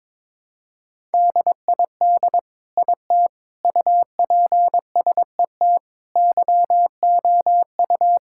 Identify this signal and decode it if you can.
Morse code, heard as a keyed tone, about 22 wpm, 700 Hz. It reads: DID IT UPSET YOU
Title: Sentences from Top 1000 Words - Review Part 5 - 22wpm